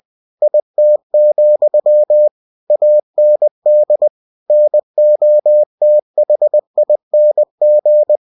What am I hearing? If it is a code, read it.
Morse code, heard as a keyed tone, about 20 wpm, 600 Hz. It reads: IT, AND NOTHING
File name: cw-books / winnie-the-pooh / winnie-20wpm-page-0032